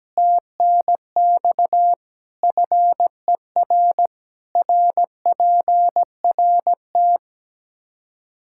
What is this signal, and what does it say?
Morse code, heard as a keyed tone, about 17 wpm, 700 Hz. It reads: TNX FER RPRT
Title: K1USN SST QSO Practice 17wpm